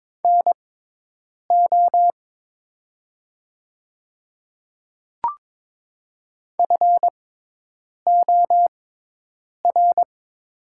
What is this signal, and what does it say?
Morse code, heard as a keyed tone, about 21 wpm, spaced as for 8 wpm, 700 Hz. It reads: NO FOR
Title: 6x Character Spacing - Top 100 Words - 22wpm